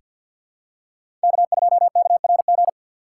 Morse code, heard as a keyed tone, about 47 wpm, 700 Hz. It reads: K2KRG
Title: USA Call Signs - 5 Characters - 2x Character Spacing - 50wpm